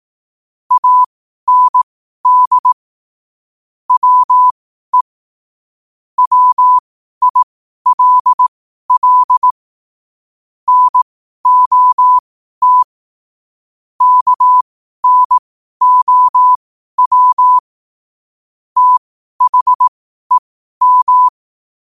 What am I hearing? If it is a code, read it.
Morse code, heard as a keyed tone, about 18 wpm, spaced as for 12 wpm, 1000 Hz. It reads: AND WE WILL NOT KNOW THEM
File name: qotdfiles / 12wpm